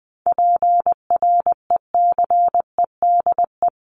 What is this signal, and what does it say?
Morse code, heard as a keyed tone, about 20 wpm, 700 Hz. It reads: PRECEDE
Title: ING Suffix - Top 100 Words with Base Words 20wpm